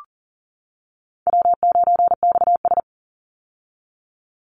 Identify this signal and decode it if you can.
Morse code, heard as a keyed tone, about 40 wpm, 700 Hz. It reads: W9XS
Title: Call Signs - 40wpm